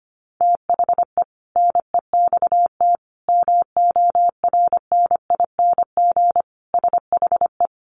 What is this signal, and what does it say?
Morse code, heard as a keyed tone, about 25 wpm, 700 Hz. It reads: THE NEXT MORNING SHE
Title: Sentences from Top 1200 Words 25wpm